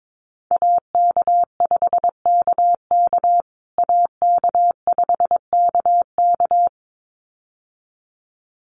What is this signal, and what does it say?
Morse code, heard as a keyed tone, about 22 wpm, 700 Hz. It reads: AK5KK AK5KK
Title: USA Call Signs Repeated 2x 22wpm